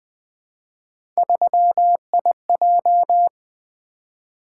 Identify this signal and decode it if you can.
Morse code, heard as a keyed tone, about 20 wpm, 700 Hz. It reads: 3IJ
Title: Three Letter-Number Combinations - 20wpm